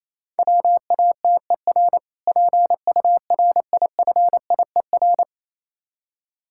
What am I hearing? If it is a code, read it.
Morse code, heard as a keyed tone, about 27 wpm, 700 Hz. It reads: WATER PURIFIER
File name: Easy Words - Speed-Racing with Hints - 22wpm